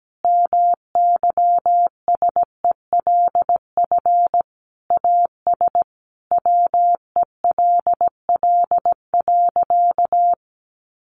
Morse code, heard as a keyed tone, about 17 wpm, 700 Hz. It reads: MYSELF AS WELL.
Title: Sentences from Top 1300 Words 17wpm